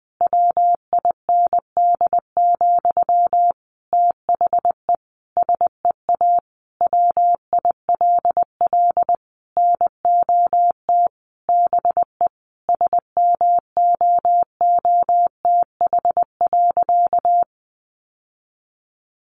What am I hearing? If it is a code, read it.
Morse code, heard as a keyed tone, about 19 wpm, 700 Hz. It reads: WIND, THE SEA WILL NOT BE SMOOTH.
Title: Sentences from Top 1400 Words 20wpm